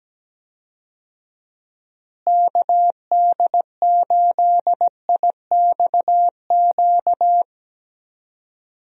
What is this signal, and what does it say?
Morse code, heard as a keyed tone, about 17 wpm, 700 Hz. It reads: KD8IXQ
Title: USA Call Signs - 6 Characters - Repeated 2x - 17wpm